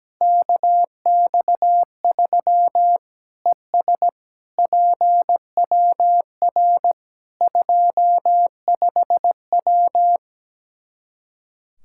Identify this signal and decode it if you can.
Morse code, heard as a keyed tone, about 17 wpm, 700 Hz. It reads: KX3 ES PWR 25W
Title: Practice QSO - Exchange 2 - Info - 17wpm